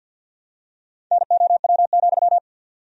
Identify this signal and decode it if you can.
Morse code, heard as a keyed tone, about 47 wpm, 700 Hz. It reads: NOW,
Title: 3x Word-Spacing - Sentences from Top 200 Words - 50wpm